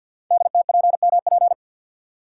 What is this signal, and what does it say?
Morse code, heard as a keyed tone, about 52 wpm, 700 Hz. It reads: DTPMP
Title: Code Groups - 5 Letters - 50wpm